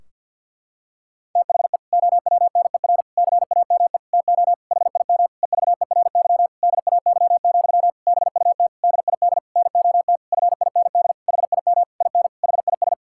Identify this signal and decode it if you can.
Morse code, heard as a keyed tone, about 50 wpm, 700 Hz. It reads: THE OWNER CAME TO HIM EVERY DAY, BUT DID NOT FIND HIM IN HIS
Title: Sentences from Top 1600 Words 50wpm